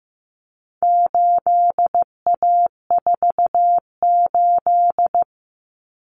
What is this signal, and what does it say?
Morse code, heard as a keyed tone, about 15 wpm, 700 Hz. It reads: 8A48
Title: Common Cut Numbers - 4-Digit 15wpm